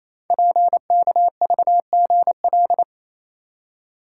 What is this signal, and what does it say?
Morse code, heard as a keyed tone, about 28 wpm, 700 Hz. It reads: PKVGL
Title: Code Groups - 5 Letters - 28wpm